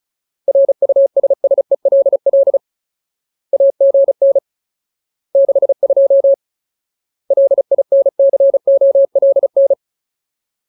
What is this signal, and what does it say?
Morse code, heard as a keyed tone, about 34 wpm, 550 Hz. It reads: RUSSELL AGN 62 LINCOLN
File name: Ultimate Word List / 35WPM Ultimate